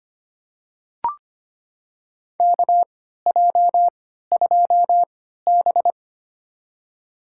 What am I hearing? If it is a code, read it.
Morse code, heard as a keyed tone, about 25 wpm, spaced as for 14 wpm, 700 Hz. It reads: KJ2B
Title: USA Call Signs - 3x Character Spacing - 25wpm